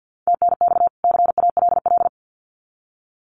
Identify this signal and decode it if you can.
Morse code, heard as a keyed tone, about 50 wpm, 700 Hz. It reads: TNX CALL
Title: Common QSO Phrases 50wpm